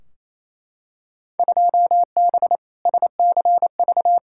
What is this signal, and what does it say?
Morse code, heard as a keyed tone, about 28 wpm, 700 Hz. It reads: 2B SCV
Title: Field Day Exchanges Encore - 28wpm